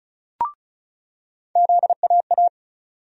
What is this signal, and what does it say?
Morse code, heard as a keyed tone, about 34 wpm, 700 Hz. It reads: ZAA
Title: USA Call Sign Postfixes - 3 Letters 35wpm